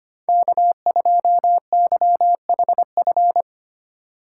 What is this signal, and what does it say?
Morse code, heard as a keyed tone, about 25 wpm, 700 Hz. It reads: K2YHF
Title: USA Call Signs - 5 Characters - 2x Character Spacing - 25wpm